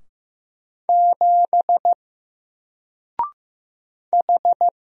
Morse code, heard as a keyed tone, about 15 wpm, 700 Hz. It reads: 7 H
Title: Single Letter-Number Alt 15wpm